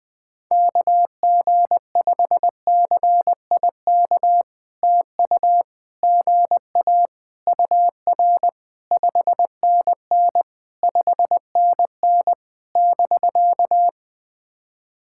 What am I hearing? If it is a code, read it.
Morse code, heard as a keyed tone, about 20 wpm, 690 Hz. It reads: KG5CIK TU GA UR 5NN 5NN <BK>
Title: POTA QSO Practice - 20wpm